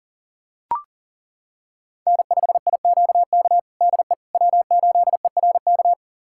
Morse code, heard as a keyed tone, about 40 wpm, 700 Hz. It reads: N5IQK DE W8ERK
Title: USA Callsign DE Callsign - 5 Characters - 40wpm